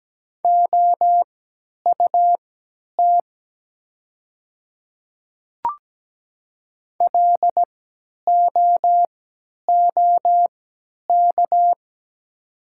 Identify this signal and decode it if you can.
Morse code, heard as a keyed tone, about 17 wpm, spaced as for 10 wpm, 700 Hz. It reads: OUT LOOK
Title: 3x Character Spacing - Top 100 Words - 17wpm